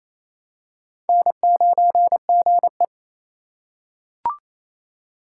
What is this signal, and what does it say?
Morse code, heard as a keyed tone, about 28 wpm, 700 Hz. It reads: N9GE
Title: USA Call Signs - 4 Characters - 6x Character Spacing - 28wpm